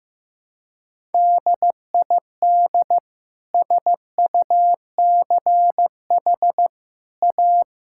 Morse code, heard as a keyed tone, about 15 wpm, 700 Hz. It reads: DID SUCH A
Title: Sentences from Top 1000 Words - Review Part 5 - 15wpm